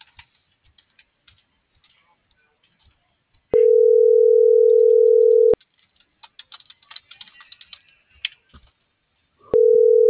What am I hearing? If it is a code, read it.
Ringback tone (North America), 440 + 480 Hz.